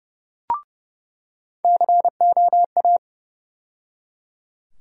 Morse code, heard as a keyed tone, about 30 wpm, 700 Hz. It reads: COA
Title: USA Call Sign Postfixes - 3 Letters 30wpm